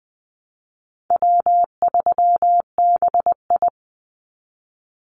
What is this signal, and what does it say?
Morse code, heard as a keyed tone, about 20 wpm, 700 Hz. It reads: W3BI
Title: Easy Call Signs Repeated 3x - 20wpm